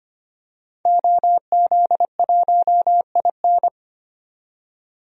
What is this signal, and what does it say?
Morse code, heard as a keyed tone, about 25 wpm, 700 Hz. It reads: OZ1IN
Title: DXCC-100 Call Signs - 5 Characters - 25wpm